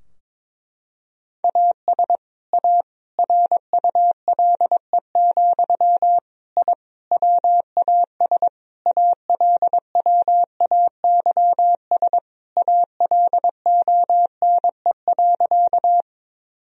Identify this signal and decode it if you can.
Morse code, heard as a keyed tone, about 22 wpm, 700 Hz. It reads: AS A RULE, I WAS ALWAYS ALONE.
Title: Sentences from Top 1000 Words - Review Part 2 - 22wpm